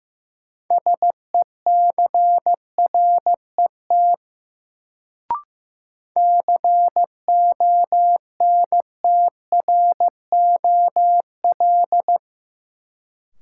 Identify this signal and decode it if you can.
Morse code, heard as a keyed tone, about 15 wpm, 700 Hz. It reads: SECRET CONTROL
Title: Sentences from Top 1200 Words - New Word Drill Alt - 15wpm